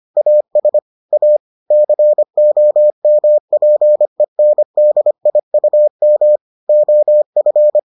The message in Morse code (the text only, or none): AS A COMPENDIUM OF